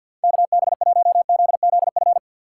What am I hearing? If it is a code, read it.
Morse code, heard as a keyed tone, about 50 wpm, 700 Hz. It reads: KB1ZZF
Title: USA Call Signs - 4x Character Spacing - 50wpm